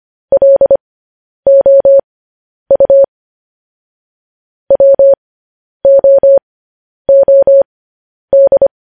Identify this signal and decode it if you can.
Morse code, heard as a keyed tone, about 25 wpm, spaced as for 10 wpm, 550 Hz. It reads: LOU WOOD
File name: podcast / News headlines-1x-25x10